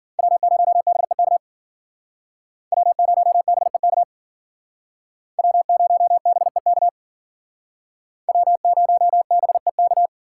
Morse code, heard as a keyed tone, about 54 wpm, 700 Hz. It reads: W0BEK W0BEK W0BEK W0BEK